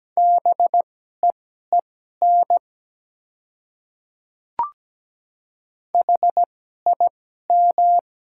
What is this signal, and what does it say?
Morse code, heard as a keyed tone, about 17 wpm, spaced as for 12 wpm, 700 Hz. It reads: BEEN HIM
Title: 2x Character Spacing - Top 100 Words - 17wpm